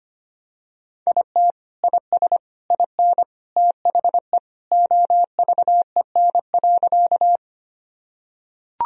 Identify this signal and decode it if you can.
Morse code, heard as a keyed tone, about 25 wpm, 700 Hz. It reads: IT IS IN THE OVEN.
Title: Sentences from Top 1000 Words - Review Part 4 - 25wpm